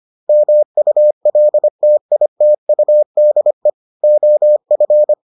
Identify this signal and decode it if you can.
Morse code, heard as a keyed tone, about 25 wpm, 600 Hz. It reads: MULTITUDE OF